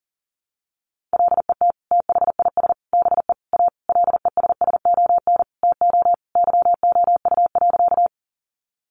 Morse code, heard as a keyed tone, about 40 wpm, 700 Hz. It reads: LET THIS BE A LESSON TO YOU.
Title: Sentences from Top 1200 Words 40wpm